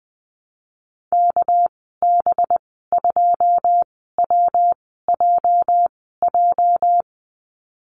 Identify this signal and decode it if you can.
Morse code, heard as a keyed tone, about 20 wpm, spaced as for 14 wpm, 700 Hz. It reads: KB2WJJ
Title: 245 Lesson 39g - Letter z - Call Signs 20wpm